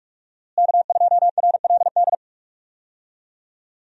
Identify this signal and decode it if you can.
Morse code, heard as a keyed tone, about 46 wpm, 700 Hz. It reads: K2RLD
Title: USA Call Signs - 5 Characters - Repeated 2x - 45wpm